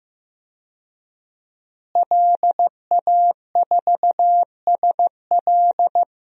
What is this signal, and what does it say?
Morse code, heard as a keyed tone, about 15 wpm, 700 Hz. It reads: LA4SL
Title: DXCC-100 Call Signs - 5 Characters - Repeated 3x - 15wpm